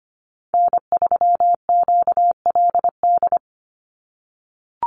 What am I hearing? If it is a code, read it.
Morse code, heard as a keyed tone, about 25 wpm, 700 Hz. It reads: N3QLD
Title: USA Call Signs 25wpm